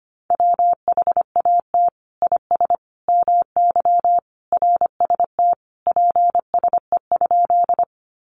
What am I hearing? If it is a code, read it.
Morse code, heard as a keyed tone, about 25 wpm, 700 Hz. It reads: WHAT IS MY RST PSE?
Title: Common QSO Phrases 25wpm